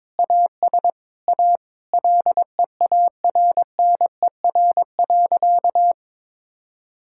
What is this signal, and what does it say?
Morse code, heard as a keyed tone, about 22 wpm, 700 Hz. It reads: AS A LEARNER.
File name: practice / 22wpm / Sentences with 8-Word Phrases - 22wpm